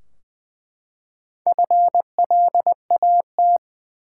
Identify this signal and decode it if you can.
Morse code, heard as a keyed tone, about 20 wpm, 700 Hz. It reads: FLAT
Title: Four Letter Words - 20wpm